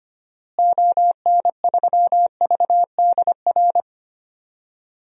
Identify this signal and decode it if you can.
Morse code, heard as a keyed tone, about 25 wpm, 700 Hz. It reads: ON3VDR
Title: DXCC-100 Call Signs - 6 Characters - Repeated 2x - 25wpm